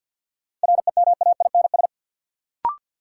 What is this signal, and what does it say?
Morse code, heard as a keyed tone, about 50 wpm, 700 Hz. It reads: REMAINS
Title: Sets of 5 Words Encore - New Word Drill - 50wpm